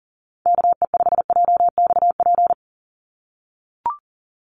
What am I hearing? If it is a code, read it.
Morse code, heard as a keyed tone, about 40 wpm, 700 Hz. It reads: KE5JXP
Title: USA Call Signs 40wpm